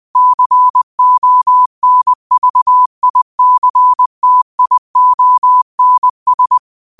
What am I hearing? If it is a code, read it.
Morse code, heard as a keyed tone, about 20 wpm, 1000 Hz. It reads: CONVICTIONS
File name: qotdfiles / 20wpm